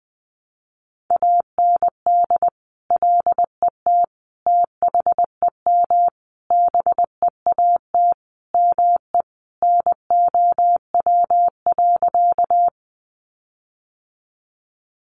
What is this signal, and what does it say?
Morse code, heard as a keyed tone, about 20 wpm, 700 Hz. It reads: AND LET THEM BEAT ME NOW.
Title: Sentences from Top 1300 Words 20wpm